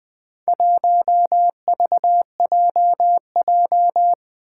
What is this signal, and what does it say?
Morse code, heard as a keyed tone, about 20 wpm, 700 Hz. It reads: 1VJJ
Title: Four Challenging Letter-Number Combinations 20wpm